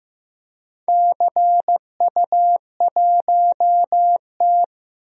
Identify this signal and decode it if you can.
Morse code, heard as a keyed tone, about 15 wpm, 700 Hz. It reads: CU1T
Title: DXCC-100 Call Signs - 4 Characters - 15wpm